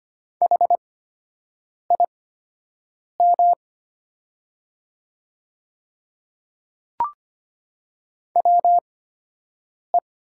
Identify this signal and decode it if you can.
Morse code, heard as a keyed tone, about 25 wpm, spaced as for 7 wpm, 700 Hz. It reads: HIM WE
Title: 8x Character Spacing - Top 300 Words - 25wpm